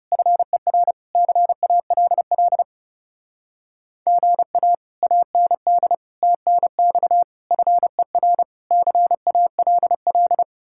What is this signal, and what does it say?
Morse code, heard as a keyed tone, about 37 wpm, 700 Hz. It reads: FER CALL GA AND TNX FER CALL
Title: Speed-Racing Common QSO Phrases - 30wpm